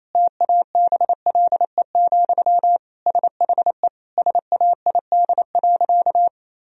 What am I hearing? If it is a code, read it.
Morse code, heard as a keyed tone, about 28 wpm, 700 Hz. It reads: TABLE, SHE SAID.
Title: Sentences from Top 1000 Words - Review Part 2 - 28wpm